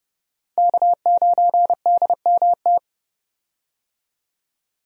Morse code, heard as a keyed tone, about 30 wpm, 700 Hz. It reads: K9DMT
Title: USA Call Signs - 5 Characters - 30wpm